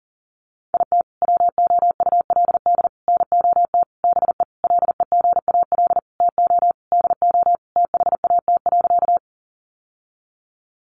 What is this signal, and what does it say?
Morse code, heard as a keyed tone, about 40 wpm, 700 Hz. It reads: IT WOULD NOT BE LEGAL TO DO THAT.
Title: Sentences from Top 1200 Words 40wpm